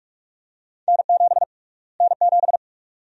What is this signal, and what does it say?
Morse code, heard as a keyed tone, about 46 wpm, 700 Hz. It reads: N7 N7
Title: USA Call Sign Prefixes - Repeated 2x - 45wpm